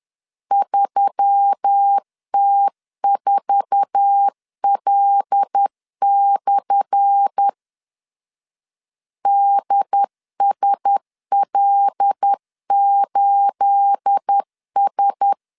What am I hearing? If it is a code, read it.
Morse code, heard as a keyed tone, about 10 wpm, 800 Hz. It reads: 3T4L/ DSL8S
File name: LEZIONI / 10 SIGNS / 10B INT 10wpm 2844